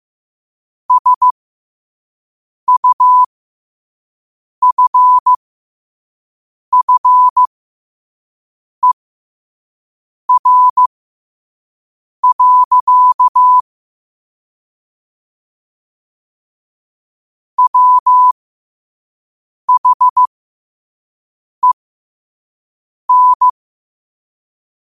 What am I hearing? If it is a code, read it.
Morse code, heard as a keyed tone, about 15 wpm, spaced as for 5 wpm, 1000 Hz. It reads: SUFFER. WHEN
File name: qotdfiles / 5wpm